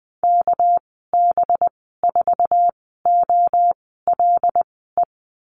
Morse code, heard as a keyed tone, about 20 wpm, spaced as for 15 wpm, 700 Hz. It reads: KB4OLE
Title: 231 Lesson 37g - Letter x - Call Signs 20wpm